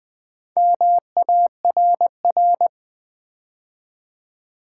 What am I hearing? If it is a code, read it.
Morse code, heard as a keyed tone, about 20 wpm, 700 Hz. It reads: MARR
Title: Build A Word - 2 Letters At A Time - 8-Letter Words - 20wpm